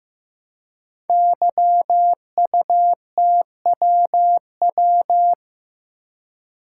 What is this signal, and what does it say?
Morse code, heard as a keyed tone, about 15 wpm, 700 Hz. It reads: YUTWW
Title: Code Groups - 5 Letters - 15wpm